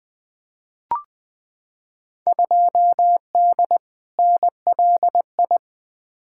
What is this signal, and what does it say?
Morse code, heard as a keyed tone, about 20 wpm, 700 Hz. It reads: 2D NLI
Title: Field Day Exchanges - 20wpm